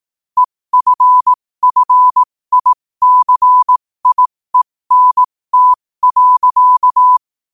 Morse code, heard as a keyed tone, about 18 wpm, spaced as for 15 wpm, 1000 Hz. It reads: EFFICIENT.
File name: qotdfiles / 15wpm